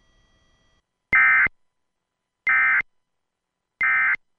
Emergency Alert System (SAME) tones, heard as harsh buzzing bursts. It is heard from 1.1 to 4.1 s.